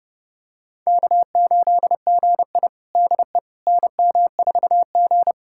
Morse code, heard as a keyed tone, about 30 wpm, 700 Hz. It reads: K8GI DE NM4G
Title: USA Callsign DE Callsign - 4 Characters - 30wpm